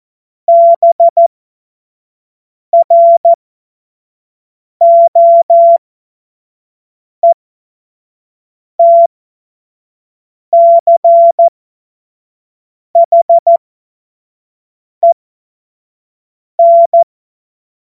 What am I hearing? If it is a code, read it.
Morse code, heard as a keyed tone, about 14 wpm, spaced as for 5 wpm, 680 Hz. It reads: BROETCHEN
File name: Morsekurs / Witze / Witz 21. Dezember 2021, 50 F70